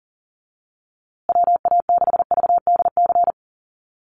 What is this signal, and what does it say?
Morse code, heard as a keyed tone, about 40 wpm, 700 Hz. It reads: WA6VDC
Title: USA Call Signs Repeated 2x 40wpm